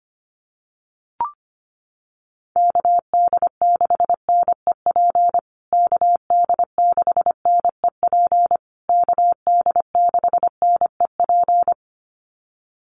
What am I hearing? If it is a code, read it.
Morse code, heard as a keyed tone, about 25 wpm, 700 Hz. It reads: KD6NEP KD6NEP KD6NEP
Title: USA Call Signs Repeated 3x 25wpm